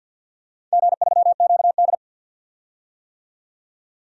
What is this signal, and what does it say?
Morse code, heard as a keyed tone, about 47 wpm, 700 Hz. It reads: G3QD
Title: DXCC-100 Call Signs - 4 Characters - 50wpm